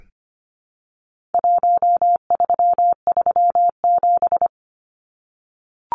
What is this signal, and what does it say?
Morse code, heard as a keyed tone, about 25 wpm, 700 Hz. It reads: 1337
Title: Numbers - 4-Digit 25wpm